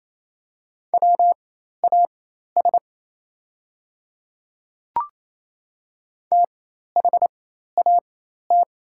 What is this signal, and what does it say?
Morse code, heard as a keyed tone, about 28 wpm, spaced as for 13 wpm, 700 Hz. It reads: WAS THAT
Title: 4x Character Spacing - Top 100 Words - 28wpm